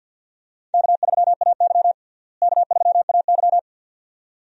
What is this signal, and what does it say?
Morse code, heard as a keyed tone, about 46 wpm, 700 Hz. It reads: K3AY K3AY
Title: USA Call Signs - 4 Characters - Repeated 2x - 50wpm